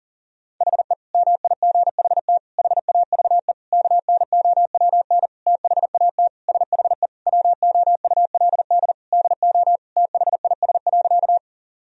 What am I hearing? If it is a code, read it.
Morse code, heard as a keyed tone, about 37 wpm, 700 Hz. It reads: HE MIGHT HAVE KNOWN THAT SHE WOULD DO THIS.
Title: Sentences from Top 1000 Words - Review Part 3 - 40wpm